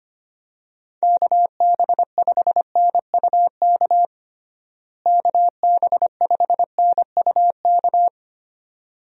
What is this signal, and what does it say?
Morse code, heard as a keyed tone, about 25 wpm, 700 Hz. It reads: KB5NUK KB5NUK
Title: USA Call Signs - 6 Characters - Repeated 2x - 25wpm